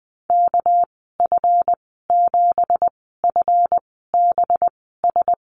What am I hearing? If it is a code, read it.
Morse code, heard as a keyed tone, about 20 wpm, spaced as for 14 wpm, 700 Hz. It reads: KF7FBS
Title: 219 Lesson 35g - Number 0 - Call Signs 20wpm